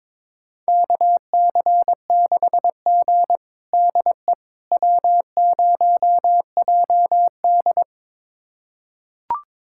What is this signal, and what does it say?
Morse code, heard as a keyed tone, about 22 wpm, 700 Hz. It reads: KC6G DE W0JD
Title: USA Callsign DE Callsign - 4 Characters - 22wpm